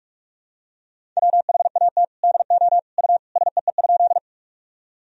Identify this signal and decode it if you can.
Morse code, heard as a keyed tone, about 46 wpm, 700 Hz. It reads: WHAT DO U SEE?
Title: Sentences from Top 100 Words and Abbreviations 45wpm